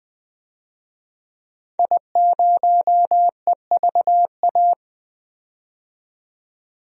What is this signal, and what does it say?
Morse code, heard as a keyed tone, about 20 wpm, 700 Hz. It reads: I0EVA
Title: DXCC-100 Call Signs - 5 Characters - 20wpm